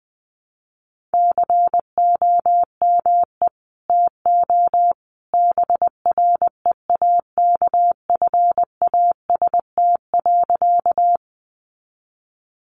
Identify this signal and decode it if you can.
Morse code, heard as a keyed tone, about 20 wpm, 700 Hz. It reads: COME TO BREAKFAST.
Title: Sentences from Top 1500 Words 20wpm